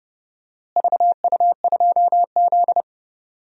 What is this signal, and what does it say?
Morse code, heard as a keyed tone, about 30 wpm, 700 Hz. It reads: VU2Z
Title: DXCC-100 Call Signs - 4 Characters - Repeated 3x - 30wpm